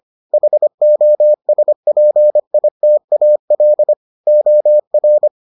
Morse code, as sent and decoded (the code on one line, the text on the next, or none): .... --- ... .--. .. - .- .-.. / --- .-.
HOSPITAL OR